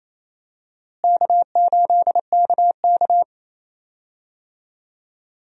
Morse code, heard as a keyed tone, about 28 wpm, 700 Hz. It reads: K8KK
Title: USA Call Signs - 4 Characters - Repeated 2x - 28wpm